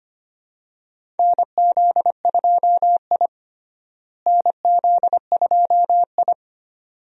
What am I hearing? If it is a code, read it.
Morse code, heard as a keyed tone, about 25 wpm, 700 Hz. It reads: NZ2I NZ2I
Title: USA Call Signs - 4 Characters - Repeated 2x - 25wpm